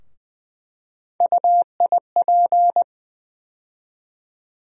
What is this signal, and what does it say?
Morse code, heard as a keyed tone, about 20 wpm, 700 Hz. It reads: UIP
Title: USA Call Sign Postfixes - 3 Letters 20wpm